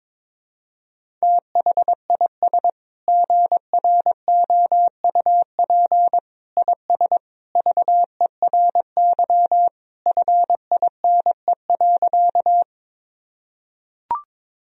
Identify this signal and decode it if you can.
Morse code, heard as a keyed tone, about 21 wpm, 700 Hz. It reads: THIS GROUP IS VERY FINE.
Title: Sentences from Top 1000 Words - Review Part 1 - 22wpm